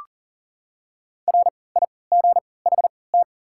Morse code, heard as a keyed tone, about 40 wpm, spaced as for 23 wpm, 700 Hz. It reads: RIGHT
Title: 3x Character Spacing - Top 200 Words - 40wpm